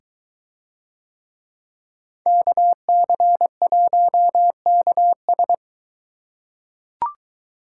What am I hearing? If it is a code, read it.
Morse code, heard as a keyed tone, about 21 wpm, 700 Hz. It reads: KC1KS